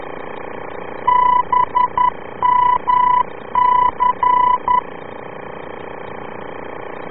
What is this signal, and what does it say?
Morse code, heard as a keyed tone, about 11 wpm, 990 Hz. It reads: BMC